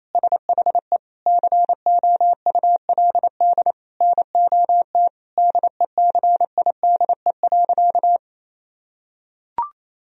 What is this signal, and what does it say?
Morse code, heard as a keyed tone, about 27 wpm, 700 Hz. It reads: SHE COULD NOT DECIDE.
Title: Sentences from Top 1000 Words - Review Part 3 - 28wpm